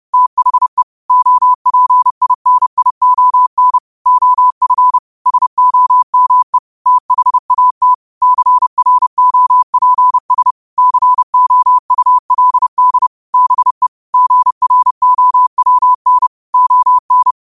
Morse code, heard as a keyed tone, about 30 wpm, 1000 Hz. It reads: THE OPINION OF SOME THAT CROPS COULD BE GROWN ON
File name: qotdfiles / 30wpm